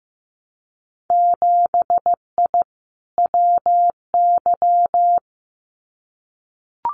Morse code, heard as a keyed tone, about 15 wpm, 700 Hz. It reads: 7I WY
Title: Winter Field Day Exchanges Encore 15wpm